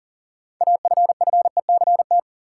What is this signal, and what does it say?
Morse code, heard as a keyed tone, about 40 wpm, 700 Hz. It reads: AFFECT
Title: Sentences from Top 900 Words - New Word Drill - 40wpm